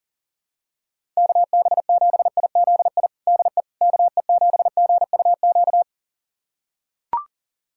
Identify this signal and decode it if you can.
Morse code, heard as a keyed tone, about 40 wpm, 700 Hz. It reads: KB7IZI DE KE7GUQ
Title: USA Callsign DE Callsign - 6 Characters - 40wpm